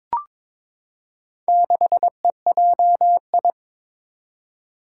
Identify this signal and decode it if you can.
Morse code, heard as a keyed tone, about 22 wpm, 700 Hz. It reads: 6EJI
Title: Four Challenging Letter-Number Combinations 22wpm